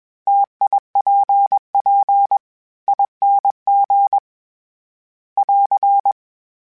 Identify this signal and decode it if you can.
Morse code, heard as a keyed tone, about 21 wpm, 800 Hz. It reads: TIPP ING <AR>
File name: Variable Spacing / Variable Spacing-mp3s / T AOEs @20